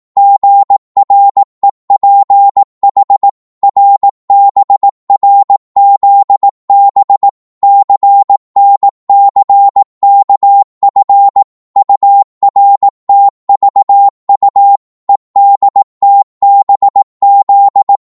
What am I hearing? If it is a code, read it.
Morse code, heard as a keyed tone, about 18 wpm, 800 Hz. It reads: GREPH RBRZB CNCKF URTVU EDTBZ